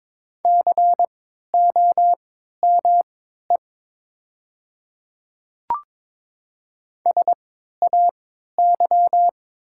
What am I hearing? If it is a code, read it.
Morse code, heard as a keyed tone, about 22 wpm, spaced as for 12 wpm, 700 Hz. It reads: COME SAY
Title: 3x Character Spacing - Top 200 Words - 22wpm